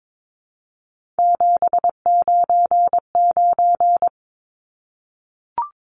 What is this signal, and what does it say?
Morse code, heard as a keyed tone, about 22 wpm, 700 Hz. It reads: 799
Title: Numbers - 3-Digit 22wpm